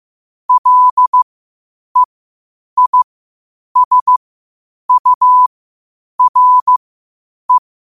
Morse code, heard as a keyed tone, about 15 wpm, spaced as for 8 wpm, 1000 Hz. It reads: LEISURE
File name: qotdfiles / 8wpm